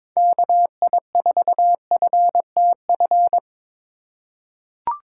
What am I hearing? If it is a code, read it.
Morse code, heard as a keyed tone, about 22 wpm, 700 Hz. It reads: KI4FTF
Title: USA Call Signs - 2x Character Spacing - 22wpm